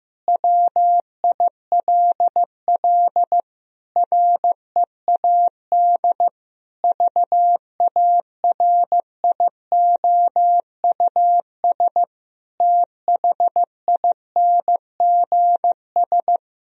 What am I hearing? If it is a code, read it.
Morse code, heard as a keyed tone, about 15 wpm, 700 Hz. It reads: WILL READ VARIOUS THINGS
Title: Sentences from Top 1000 Words - Review Part 4 - 15wpm